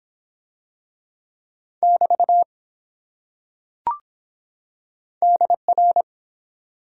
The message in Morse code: <BT> DR

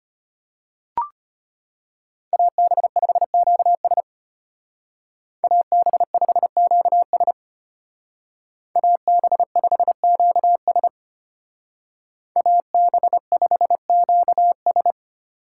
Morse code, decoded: AB5QS AB5QS AB5QS AB5QS

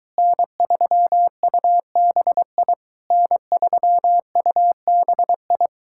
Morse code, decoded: N3UBI N3UBI

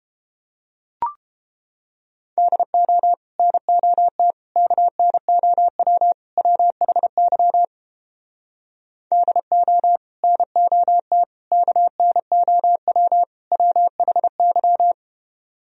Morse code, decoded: DO NOT KNOW WHY DO NOT KNOW WHY

